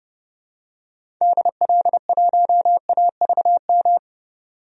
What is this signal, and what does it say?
Morse code, heard as a keyed tone, about 30 wpm, 700 Hz. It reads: DL1AVM
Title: DXCC-100 Call Signs - 6 Characters - 30wpm